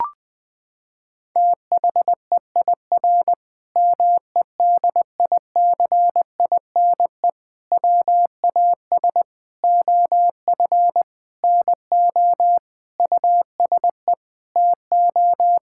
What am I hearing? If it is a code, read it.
Morse code, heard as a keyed tone, about 20 wpm, 700 Hz. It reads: THEIR MEDICINE WAS OF NO USE TO
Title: Sentences from Top 1000 Words - Review Part 6 - 20wpm